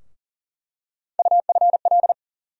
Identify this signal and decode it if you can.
Morse code, heard as a keyed tone, about 40 wpm, 700 Hz. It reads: UFL
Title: USA Call Sign Postfixes - 3 Letters 40wpm